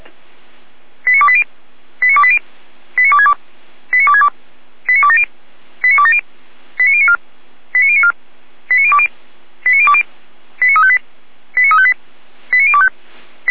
Five-tone selective call (ZVEI), heard as quick runs of five tones, tones of about 70 ms. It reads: 89280 89280 89262 89262 89270 89270 89094 89094 89020 89020 89358 89358 89025